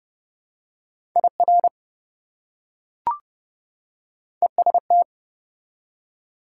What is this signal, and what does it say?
Morse code, heard as a keyed tone, about 30 wpm, 700 Hz. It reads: IR EST
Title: Prefixes and Suffixes 30wpm